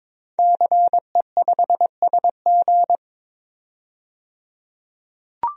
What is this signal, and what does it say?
Morse code, heard as a keyed tone, about 22 wpm, 700 Hz. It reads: CE5SG
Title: DXCC-100 Call Signs - 5 Characters - 22wpm